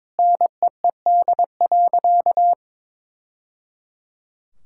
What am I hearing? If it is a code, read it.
Morse code, heard as a keyed tone, about 22 wpm, 700 Hz. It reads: NEED.
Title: Sentences from Top 1000 Words - Review Part 6 - 22wpm